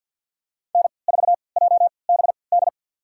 Morse code, heard as a keyed tone, about 46 wpm, spaced as for 27 wpm, 700 Hz. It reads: N4JBD
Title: USA Call Signs - 3x Character Spacing - 50wpm